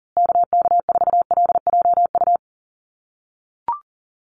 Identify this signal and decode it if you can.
Morse code, heard as a keyed tone, about 40 wpm, 700 Hz. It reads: KK4LJU
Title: USA Call Signs Repeated 2x 40wpm